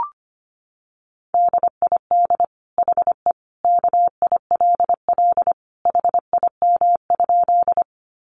Morse code, heard as a keyed tone, about 25 wpm, 700 Hz. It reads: DID HE KILL HIM?
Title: Sentences from Top 1400 Words 25wpm